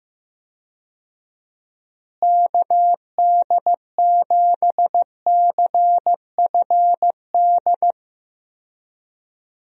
Morse code, heard as a keyed tone, about 14 wpm, 700 Hz. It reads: KD7CFD
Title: USA Call Signs - 6 Characters - Repeated 3x - 15wpm